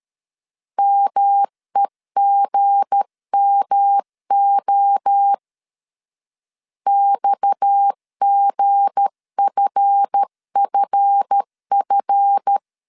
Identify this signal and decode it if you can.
Morse code, heard as a keyed tone, about 13 wpm, 800 Hz. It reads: MEGMO XGFFF
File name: LEZIONI / 7 GF9Z / 7 GF 12wpm 2122